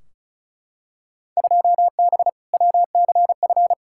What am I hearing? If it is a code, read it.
Morse code, heard as a keyed tone, about 36 wpm, 690 Hz. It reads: 2B WCF
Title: Field Day Exchanges Encore - 35wpm